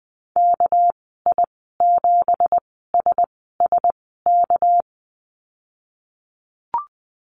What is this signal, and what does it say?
Morse code, heard as a keyed tone, about 20 wpm, spaced as for 15 wpm, 700 Hz. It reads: KI7SSK
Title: 245 Lesson 39g - Letter z - Call Signs 20wpm